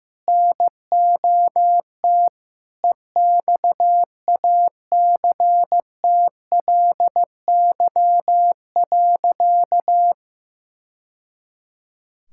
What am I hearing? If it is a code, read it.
Morse code, heard as a keyed tone, about 16 wpm, 700 Hz. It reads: NOT EXACTLY.
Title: Sentences from Top 1000 Words - Review Part 6 - 15wpm